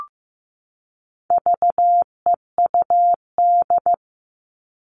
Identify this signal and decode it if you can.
Morse code, heard as a keyed tone, about 15 wpm, 700 Hz. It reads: VEUD
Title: All Cut Numbers - 4-Digit 15wpm